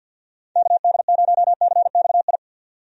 Morse code, heard as a keyed tone, about 52 wpm, 700 Hz. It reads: KD0CXI
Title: USA Call Signs - 6 Characters - 50wpm